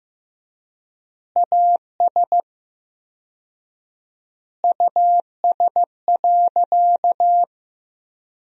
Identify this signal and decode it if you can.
Morse code, heard as a keyed tone, about 15 wpm, 700 Hz. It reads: AS US.
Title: 4x Word-Spacing - Sentences from Top 500 Words - 15wpm